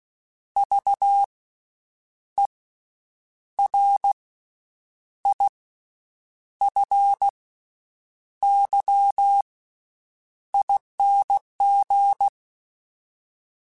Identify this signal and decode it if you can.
Morse code, heard as a keyed tone, about 16 wpm, spaced as for 7 wpm, 800 Hz. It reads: VERIFYING
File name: Special INGs / Special INGs-mp3s / VerySlowToFaster Ing @15